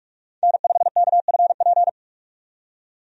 Morse code, heard as a keyed tone, about 43 wpm, 700 Hz. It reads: N5KFP
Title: USA Call Signs - 5 Characters - 45wpm